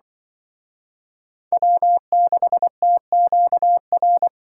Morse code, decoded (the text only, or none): W6TQR